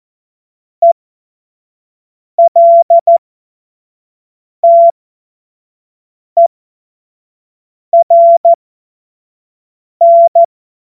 Morse code, heard as a keyed tone, about 14 wpm, spaced as for 5 wpm, 680 Hz. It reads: ELTERN
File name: Morsekurs / Witze / Witz 14. Februar 2023 25 F70